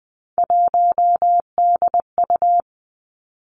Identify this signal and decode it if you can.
Morse code, heard as a keyed tone, about 20 wpm, 700 Hz. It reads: 1DU
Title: Three Challenging Letter-Number Combinations 20wpm